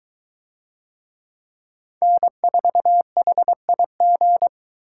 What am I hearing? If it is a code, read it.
Morse code, heard as a keyed tone, about 21 wpm, 700 Hz. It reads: N4HIG